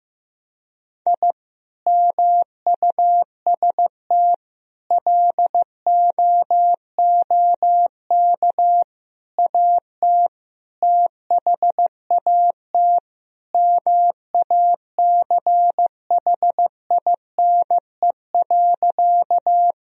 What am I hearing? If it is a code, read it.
Morse code, heard as a keyed tone, about 14 wpm, 700 Hz. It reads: I MUST LOOK AT THAT MACHINE.
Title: Sentences from Top 1000 Words - Review Part 2 - 15wpm